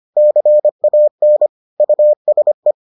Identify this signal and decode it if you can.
Morse code, heard as a keyed tone, about 25 wpm, 600 Hz. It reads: CAN USE